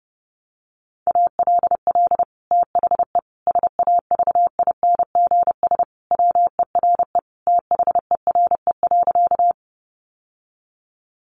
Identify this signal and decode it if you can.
Morse code, heard as a keyed tone, about 30 wpm, 700 Hz. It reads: ALL THE SAVINGS WERE THERE.
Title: Sentences from Top 1200 Words 30wpm